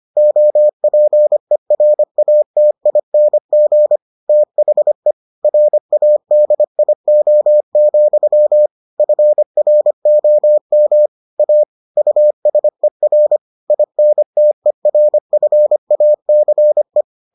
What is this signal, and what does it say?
Morse code, heard as a keyed tone, about 25 wpm, 600 Hz. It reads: OPERATING THE RADIO, FROM A USER INTERFACE